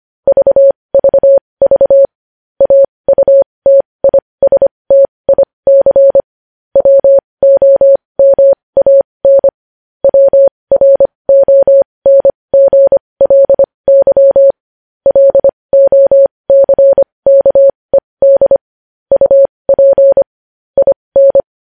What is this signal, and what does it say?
Morse code, heard as a keyed tone, about 25 wpm, spaced as for 20 wpm, 550 Hz. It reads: VVV AUTISTIC WOMAN WRONGLY LOCKED UP IN
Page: news-headlines-in-morse-code-at-20-wpm